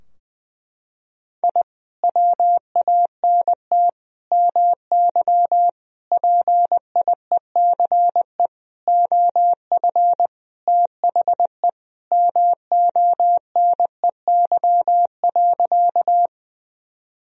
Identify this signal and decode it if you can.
Morse code, heard as a keyed tone, about 20 wpm, 700 Hz. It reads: I WANT MY PIECE OF THE MONEY.
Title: Sentences from Top 1000 Words - Review Part 2 - 20wpm